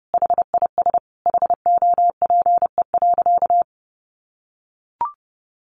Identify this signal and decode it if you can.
Morse code, heard as a keyed tone, about 30 wpm, 700 Hz. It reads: HIS HOPE.
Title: Sentences from Top 1100 Words 30wpm